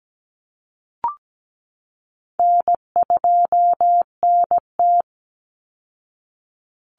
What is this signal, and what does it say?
Morse code, heard as a keyed tone, about 17 wpm, 700 Hz. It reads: N2NT
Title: Easy Call Signs Repeated 2x - 17wpm